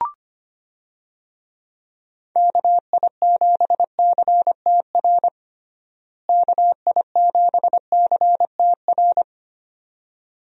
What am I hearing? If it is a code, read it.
Morse code, heard as a keyed tone, about 25 wpm, 700 Hz. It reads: KI7CTR KI7CTR
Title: USA Call Signs - 6 Characters - Repeated 2x - 25wpm